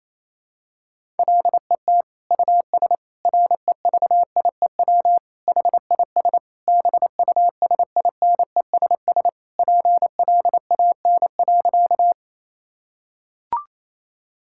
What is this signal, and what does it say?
Morse code, heard as a keyed tone, about 28 wpm, 700 Hz. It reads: LET US REVIEW HIS BUSINESS PLAN.
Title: Sentences from Top 1000 Words - Review Part 6 - 28wpm